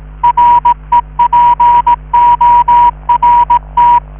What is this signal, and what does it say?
Morse code, heard as a keyed tone, about 16 wpm, 960 Hz. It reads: REPORT